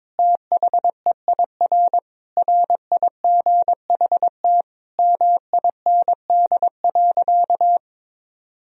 Morse code, heard as a keyed tone, about 21 wpm, 700 Hz. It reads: THEIR RIGHT MIND.
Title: Sentences from Top 1000 Words - Review Part 1 - 22wpm